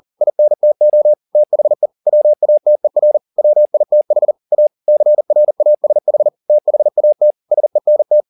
Morse code, heard as a keyed tone, about 40 wpm, 600 Hz. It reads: INTO THE WATER WITH A CRASH THAT SENT
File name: the old man and the sea 2 40wpm